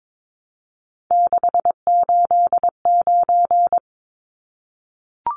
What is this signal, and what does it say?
Morse code, heard as a keyed tone, about 22 wpm, 700 Hz. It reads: 689
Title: Numbers - 3-Digit 22wpm